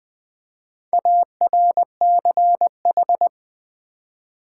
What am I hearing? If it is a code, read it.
Morse code, heard as a keyed tone, about 20 wpm, 700 Hz. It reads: ARCH
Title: Build A Word - 4 Letters At A Time - 12-Letter Words - 20wpm